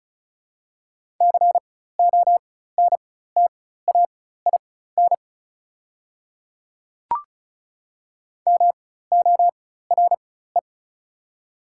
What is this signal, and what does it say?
Morse code, heard as a keyed tone, about 36 wpm, spaced as for 16 wpm, 690 Hz. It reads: CONTAIN MORE
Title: 4x Character Spacing - Top 500 Words - 35wpm